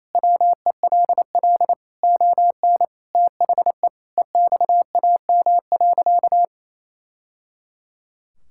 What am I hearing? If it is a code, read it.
Morse code, heard as a keyed tone, about 28 wpm, 700 Hz. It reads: WELL ON THE EXAM.
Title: Sentences from Top 1000 Words - Review Part 5 - 28wpm